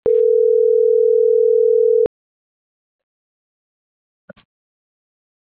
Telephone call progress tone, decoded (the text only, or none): ringback tone